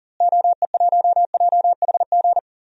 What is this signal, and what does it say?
Morse code, heard as a keyed tone, about 39 wpm, 700 Hz. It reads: OE1JHG
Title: DXCC-100 Call Signs - 6 Characters - Repeated 3x - 40wpm